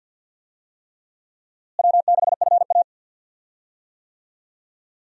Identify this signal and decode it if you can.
Morse code, heard as a keyed tone, about 51 wpm, 700 Hz. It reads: W6FA
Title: USA Call Signs - 4 Characters - Repeated 3x - 50wpm